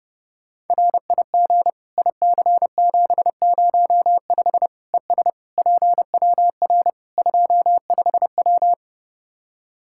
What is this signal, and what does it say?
Morse code, heard as a keyed tone, about 32 wpm, 700 Hz. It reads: RIG IC705 ES PWR 25W
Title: Practice QSO - Exchange 2 - Info - 30wpm